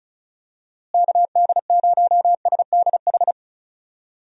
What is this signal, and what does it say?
Morse code, heard as a keyed tone, about 34 wpm, 700 Hz. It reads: KD0SDH
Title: USA Call Signs - 6 Characters - Repeated 3x - 35wpm